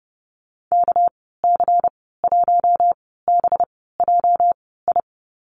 Morse code, heard as a keyed tone, about 30 wpm, spaced as for 17 wpm, 700 Hz. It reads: KC1BJI
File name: Lesson 37g - Letter x - Call Signs 30wpm